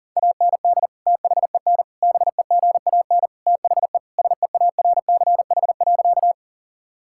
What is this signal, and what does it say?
Morse code, heard as a keyed tone, about 40 wpm, 700 Hz. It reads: AND THEN BEGAN THE SEARCH.
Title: Sentences from Top 1000 Words - Review Part 5 - 40wpm